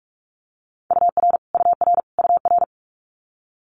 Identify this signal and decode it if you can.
Morse code, heard as a keyed tone, about 46 wpm, 700 Hz. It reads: UR UR UR
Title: Multi-letter Phonemes Repeated 3x 45wpm